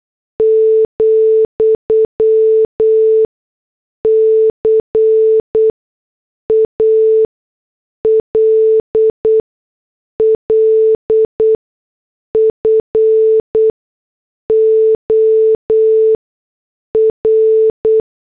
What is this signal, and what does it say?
Morse code, heard as a keyed tone, about 8 wpm, spaced as for 6 wpm, 440 Hz. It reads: , CALL FOR
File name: morse / headlines